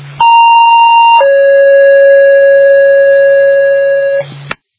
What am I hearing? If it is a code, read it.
Two-tone page, heard as a one-second tone, then a three-second tone, about 930 then 550 Hz. It is heard from 0.2 to 4.2 s.